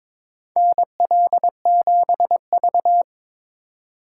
Morse code, heard as a keyed tone, about 22 wpm, 700 Hz. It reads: NL7V
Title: K1USN SST QSO Practice - 22wpm